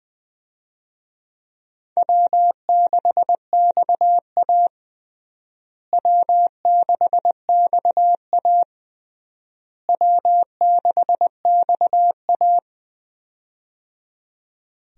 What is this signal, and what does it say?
Morse code, heard as a keyed tone, about 20 wpm, 700 Hz. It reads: W6XA W6XA W6XA
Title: USA Call Signs - 4 Characters - Repeated 3x - 20wpm